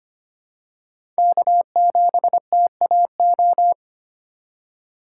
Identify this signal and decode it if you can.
Morse code, heard as a keyed tone, about 25 wpm, 700 Hz. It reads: K7TAO
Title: USA Call Signs - 5 Characters - 4x Character Spacing - 25wpm